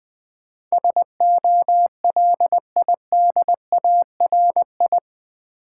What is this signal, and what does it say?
Morse code, heard as a keyed tone, about 20 wpm, 700 Hz. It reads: SOLIDARI
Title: Build A Word - 4 Letters At A Time - 10-Letter Words - 20wpm